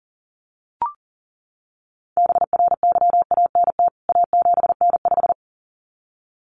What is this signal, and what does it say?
Morse code, heard as a keyed tone, about 40 wpm, 700 Hz. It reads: BRYANT A7N5
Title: CWT Exchange 40wpm